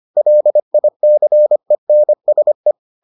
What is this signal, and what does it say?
Morse code, heard as a keyed tone, about 25 wpm, 600 Hz. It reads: LICENSE